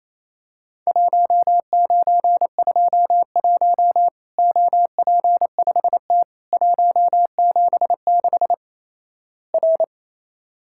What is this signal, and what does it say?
Morse code, heard as a keyed tone, about 28 wpm, 700 Hz. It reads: 1921 OP5T 176 R
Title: Worked All Europe DX Contest - QTC Exchange - 28wpm